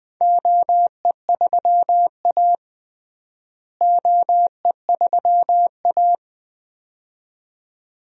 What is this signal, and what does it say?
Morse code, heard as a keyed tone, about 20 wpm, 700 Hz. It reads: OE3A OE3A
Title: DXCC-100 Call Signs - 4 Characters - Repeated 2x - 20wpm